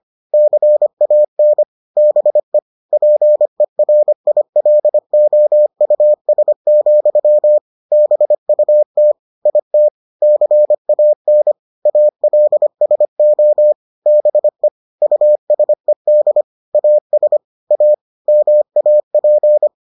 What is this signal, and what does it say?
Morse code, heard as a keyed tone, about 25 wpm, 600 Hz. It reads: CAN BE PERILOUS, BUT IT CAN ALSO BE USED AS A MAP